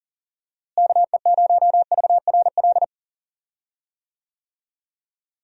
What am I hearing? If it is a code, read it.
Morse code, heard as a keyed tone, about 40 wpm, 700 Hz. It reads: KE0VRL
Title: USA Call Signs - 6 Characters - Repeated 3x - 40wpm